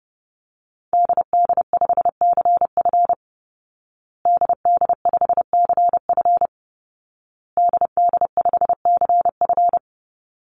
Morse code, heard as a keyed tone, about 30 wpm, 700 Hz. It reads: DD5CF DD5CF DD5CF
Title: Call Signs Repeated 2x - 30wpm